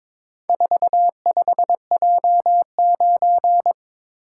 Morse code, heard as a keyed tone, about 22 wpm, 700 Hz. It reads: 45J9
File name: practice / 22wpm / Four Letter-Number Combinations - 22wpm